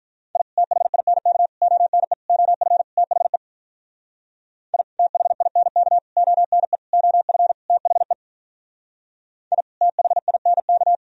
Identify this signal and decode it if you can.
Morse code, heard as a keyed tone, about 46 wpm, 700 Hz. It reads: I THINK ONE OF THE I THINK ONE OF THE I THINK